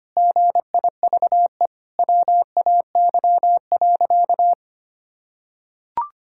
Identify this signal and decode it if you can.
Morse code, heard as a keyed tone, about 25 wpm, 700 Hz. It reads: GIVE WAY.
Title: Sentences from Top 1600 Words 25wpm